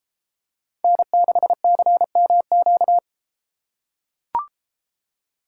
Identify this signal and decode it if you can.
Morse code, heard as a keyed tone, about 32 wpm, 700 Hz. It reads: N6CMQ